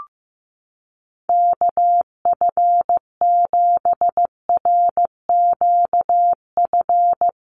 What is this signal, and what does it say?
Morse code, heard as a keyed tone, about 15 wpm, 700 Hz. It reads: KF7RQF
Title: USA Call Signs Repeated 2x 15wpm